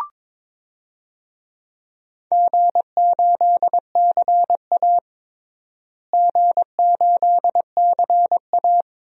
Morse code, heard as a keyed tone, about 22 wpm, 700 Hz. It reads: G8CA G8CA
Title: DXCC-100 Call Signs - 4 Characters - Repeated 2x - 22wpm